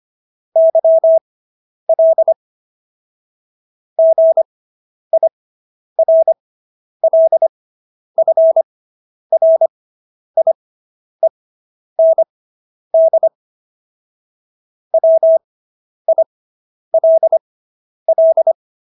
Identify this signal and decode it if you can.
Morse code, heard as a keyed tone, about 25 wpm, spaced as for 10 wpm, 650 Hz. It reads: YL GIRLFRIEND WILL